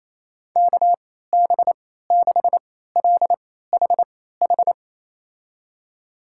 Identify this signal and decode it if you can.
Morse code, heard as a keyed tone, about 28 wpm, spaced as for 16 wpm, 700 Hz. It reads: KB6LHH
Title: USA Call Signs - 6 Characters - 3x Character Spacing - 28wpm